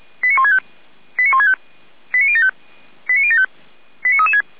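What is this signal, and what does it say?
Five-tone selective call (ZVEI), heard as quick runs of five tones, tones of about 80 ms. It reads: 89165 89165 89075 89075 89336
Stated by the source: ZVEI five-tone selective call received on SDR.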